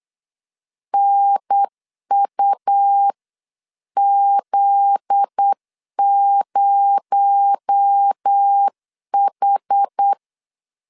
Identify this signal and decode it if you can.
Morse code, heard as a keyed tone, about 8 wpm, 800 Hz. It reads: NUZ0H
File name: CW / DET / 1 9 1500C S A 8wpm 5637